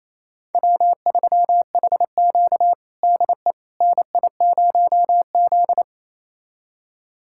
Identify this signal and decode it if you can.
Morse code, heard as a keyed tone, about 28 wpm, 700 Hz. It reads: W3HQ DE NI0Z
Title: USA Callsign DE Callsign - 4 Characters - 28wpm